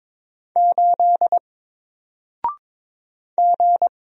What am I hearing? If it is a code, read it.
Morse code, heard as a keyed tone, about 22 wpm, 700 Hz. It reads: 8 G
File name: Single-Letter-Number-22wpm